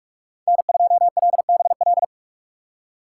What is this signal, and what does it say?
Morse code, heard as a keyed tone, about 46 wpm, 700 Hz. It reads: N2LBL
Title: USA Call Signs - 2x Character Spacing - 45wpm